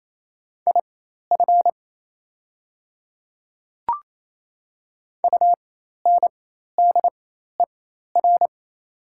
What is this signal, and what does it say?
Morse code, heard as a keyed tone, about 28 wpm, spaced as for 13 wpm, 700 Hz. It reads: IF UNDER
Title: 4x Character Spacing - Top 500 Words - 28wpm